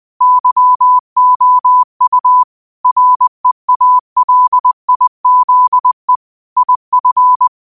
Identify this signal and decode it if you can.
Morse code, heard as a keyed tone, about 20 wpm, 1000 Hz. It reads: YOU REALIZE IF